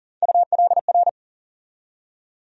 Morse code, heard as a keyed tone, about 40 wpm, 700 Hz. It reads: ULR
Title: USA Call Sign Postfixes - 3 Letters 40wpm